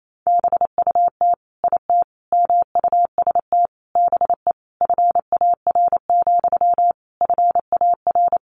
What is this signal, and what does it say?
Morse code, heard as a keyed tone, about 28 wpm, 700 Hz. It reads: BUT IT MUST BE FAR, FAR
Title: Sentences from Top 600 Words 28wpm